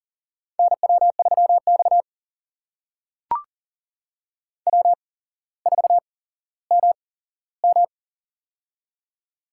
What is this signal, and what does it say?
Morse code, heard as a keyed tone, about 40 wpm, spaced as for 17 wpm, 700 Hz. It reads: NW3X W4MM
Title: USA Call Signs - 4 Characters - 8x Character Spacing - 40wpm